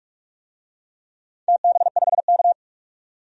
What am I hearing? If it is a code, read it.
Morse code, heard as a keyed tone, about 44 wpm, 700 Hz. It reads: TB5K
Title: DXCC-100 Call Signs - 4 Characters - 45wpm